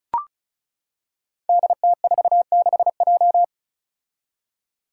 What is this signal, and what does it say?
Morse code, heard as a keyed tone, about 36 wpm, 700 Hz. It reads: DT46J
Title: Five Challenging Letter-Number Combinations 35wpm